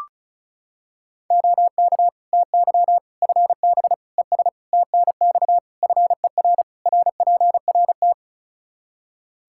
Morse code, heard as a keyed tone, about 34 wpm, 700 Hz. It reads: OK TY FB ES TNX FER RPRT
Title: Practice QSO - Exchange 2 - Info - 35wpm